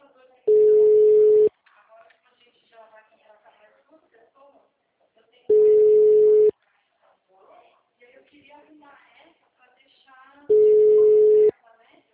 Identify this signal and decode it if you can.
Ringback tone (Europe, 425 Hz), with speech.